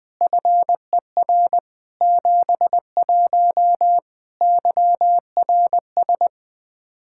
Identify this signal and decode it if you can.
Morse code, heard as a keyed tone, about 21 wpm, 700 Hz. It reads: FER 71 YRS
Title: K1USN SST QSO Practice 20wpm